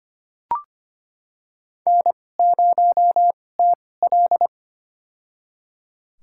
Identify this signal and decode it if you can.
Morse code, heard as a keyed tone, about 25 wpm, spaced as for 18 wpm, 700 Hz. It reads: N0TL
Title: USA Call Signs - 4 Characters - 2x Character Spacing - 25wpm